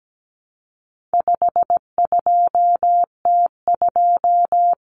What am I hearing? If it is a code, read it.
Morse code, heard as a keyed tone, about 17 wpm, 700 Hz. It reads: 52T2
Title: Common Cut Numbers - 4-Digit 17wpm